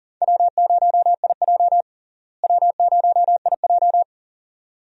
Morse code, heard as a keyed tone, about 40 wpm, 700 Hz. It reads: W0IJ W0IJ
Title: USA Call Signs - 4 Characters - Repeated 2x - 40wpm